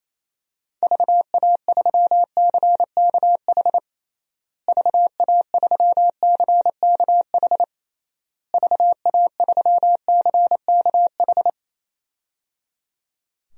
Morse code, heard as a keyed tone, about 28 wpm, 700 Hz. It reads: VA3CKH VA3CKH VA3CKH
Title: DXCC-100 Call Signs - 6 Characters - Repeated 3x - 28wpm